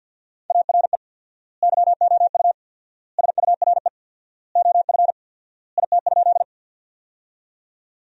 Morse code, heard as a keyed tone, about 53 wpm, 700 Hz. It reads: ARE YOU SURE OF IT?
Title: 4x Word-Spacing - Sentences from Top 300 Words - 50wpm